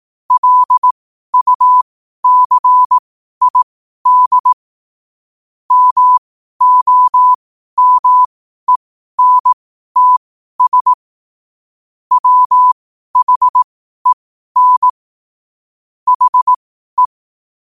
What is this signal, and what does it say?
Morse code, heard as a keyed tone, about 18 wpm, spaced as for 12 wpm, 1000 Hz. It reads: LUCID MOMENTS WHEN HE
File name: 12wpm